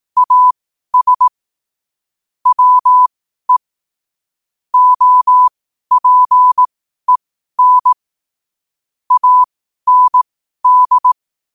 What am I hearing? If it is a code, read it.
Morse code, heard as a keyed tone, about 18 wpm, spaced as for 12 wpm, 1000 Hz. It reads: AS WE OPEN AND
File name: qotdfiles / 12wpm